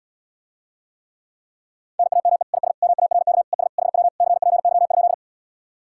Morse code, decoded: NAME IS MARK ES HW CPY?

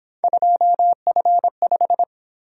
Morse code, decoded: 2F5

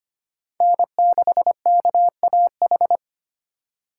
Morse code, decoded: N6KAH

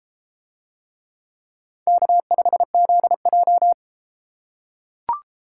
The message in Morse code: K5ZJ

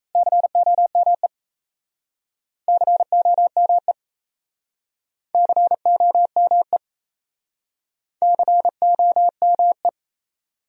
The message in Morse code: COME COME COME COME

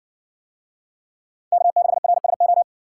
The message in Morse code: K6RSO

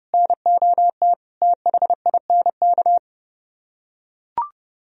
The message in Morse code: NOT THINK

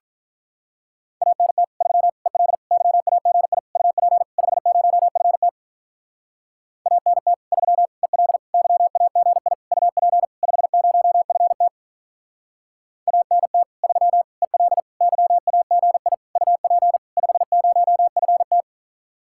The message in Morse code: ANT 3 EL YAGI UP 50FT ANT 3 EL YAGI UP 50FT ANT 3 EL YAGI UP 50FT

